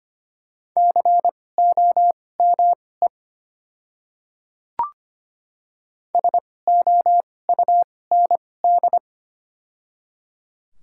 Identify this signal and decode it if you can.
Morse code, heard as a keyed tone, about 25 wpm, spaced as for 18 wpm, 700 Hz. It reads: COME SOUND
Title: 2x Character Spacing - Top 100 Words - 25wpm